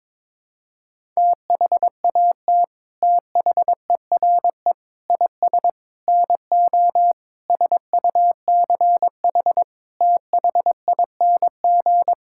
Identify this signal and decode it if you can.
Morse code, heard as a keyed tone, about 22 wpm, 700 Hz. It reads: THAT THERE IS NO SUCH THING
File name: practice / 22wpm / Sets of 8 Words - 22wpm